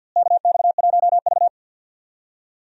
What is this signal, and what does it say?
Morse code, heard as a keyed tone, about 50 wpm, 700 Hz. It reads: KX1V
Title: USA Call Signs - 4 Characters - Repeated 3x - 50wpm